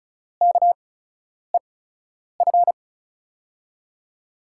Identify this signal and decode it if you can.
Morse code, heard as a keyed tone, about 36 wpm, spaced as for 10 wpm, 700 Hz. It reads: KEF
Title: USA Call Signs - 6 Characters - 8x Character Spacing - 35wpm